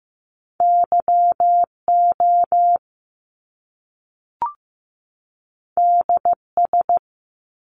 Morse code, heard as a keyed tone, about 15 wpm, 700 Hz. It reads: YO DS
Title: Top Two Letter Combinations in Words 15wpm